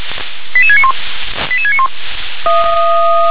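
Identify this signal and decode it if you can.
Five-tone selective call (ZVEI), heard as quick runs of five tones, tones of about 70 ms. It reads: 88691 88691
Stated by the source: ZVEI five-tone call received on SDR.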